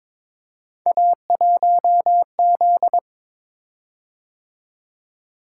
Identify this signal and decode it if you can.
Morse code, heard as a keyed tone, about 22 wpm, 700 Hz. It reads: A1Z
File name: practice / 22wpm / Three Letter-Number Combinations - 22wpm